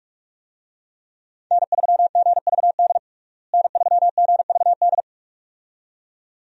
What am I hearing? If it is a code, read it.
Morse code, heard as a keyed tone, about 42 wpm, 700 Hz. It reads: N3GVD N3GVD
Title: USA Call Signs - 5 Characters - Repeated 2x - 45wpm